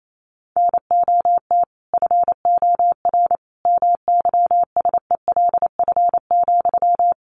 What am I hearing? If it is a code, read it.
Morse code, heard as a keyed tone, about 28 wpm, 700 Hz. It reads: NOT FOR MYSELF,
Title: Sentences from Top 1300 Words 28wpm